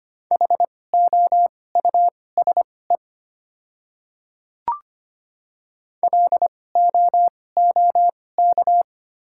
Morse code, heard as a keyed tone, about 25 wpm, spaced as for 18 wpm, 700 Hz. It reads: HOUSE LOOK
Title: 2x Character Spacing - Top 200 Words - 25wpm